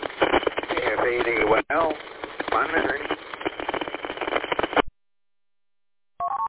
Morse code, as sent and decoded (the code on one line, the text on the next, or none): none